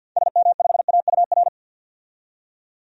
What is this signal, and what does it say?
Morse code, heard as a keyed tone, about 52 wpm, 700 Hz. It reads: SM5AUR
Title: DXCC-100 Call Signs - 6 Characters - 50wpm